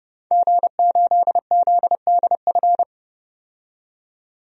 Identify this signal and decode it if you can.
Morse code, heard as a keyed tone, about 31 wpm, 700 Hz. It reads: G8ZDF
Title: DXCC-100 Call Signs - 5 Characters - 30wpm